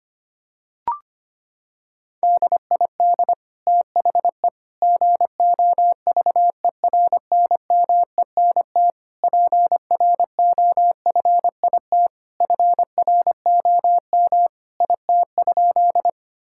Morse code, heard as a keyed tone, about 25 wpm, 700 Hz. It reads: DID THE GOVERNMENT PROFIT FROM IT?
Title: Sentences from Top 1000 Words - Review Part 5 - 25wpm